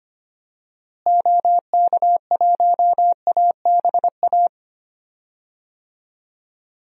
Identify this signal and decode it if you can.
Morse code, heard as a keyed tone, about 25 wpm, 700 Hz. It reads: OK1ABA
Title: DXCC-100 Call Signs - 6 Characters - Repeated 3x - 25wpm